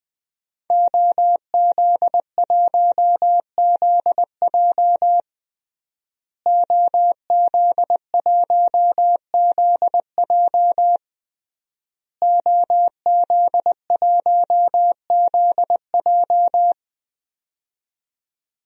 Morse code, heard as a keyed tone, about 20 wpm, 700 Hz. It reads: OZ1ZJ OZ1ZJ OZ1ZJ
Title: DXCC-100 Call Signs - 5 Characters - Repeated 3x - 20wpm